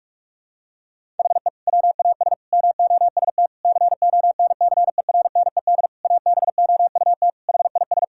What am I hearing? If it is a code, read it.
Morse code, heard as a keyed tone, about 46 wpm, 700 Hz. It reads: HE WAS MOST CONCERNED ABOUT HIS
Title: Sentences from Top 1000 Words - Review Part 6 - 45wpm